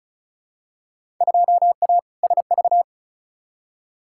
Morse code, heard as a keyed tone, about 36 wpm, 700 Hz. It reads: 2A SV
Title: Field Day Exchanges - 35wpm